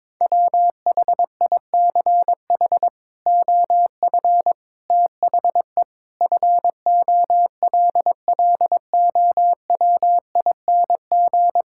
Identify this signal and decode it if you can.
Morse code, heard as a keyed tone, about 22 wpm, 700 Hz. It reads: WHICH OF THE FOLLOWING
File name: practice / 22wpm / Sets of 8 Words - 22wpm